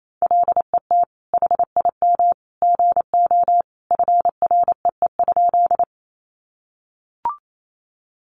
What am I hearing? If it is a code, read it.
Morse code, heard as a keyed tone, about 28 wpm, 700 Hz. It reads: LET HIM GO FREE?
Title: Sentences from Top 500 Words 28wpm